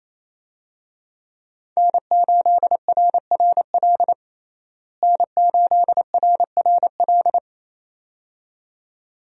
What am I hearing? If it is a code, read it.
Morse code, heard as a keyed tone, about 28 wpm, 700 Hz. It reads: N8RRL N8RRL
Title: USA Call Signs - 5 Characters - Repeated 2x - 28wpm